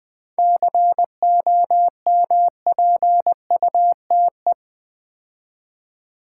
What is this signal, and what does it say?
Morse code, heard as a keyed tone, about 20 wpm, 700 Hz. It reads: COMPUTE
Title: 7 Letter Words - First 3 Letters Given - 20wpm